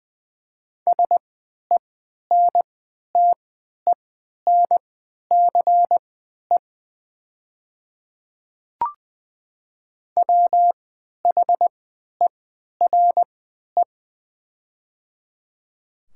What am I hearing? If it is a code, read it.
Morse code, heard as a keyed tone, about 20 wpm, spaced as for 11 wpm, 700 Hz. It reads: SENTENCE WHERE
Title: 3x Character Spacing - Top 300 Words - 20wpm